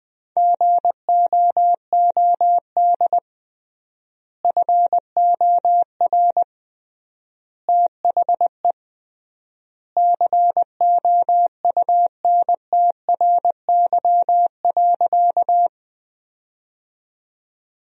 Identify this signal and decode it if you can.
Morse code, heard as a keyed tone, about 20 wpm, 700 Hz. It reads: GOOD FOR THE COUNTRY.
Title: 3x Word-Spacing - Sentences from Top 300 Words - 20wpm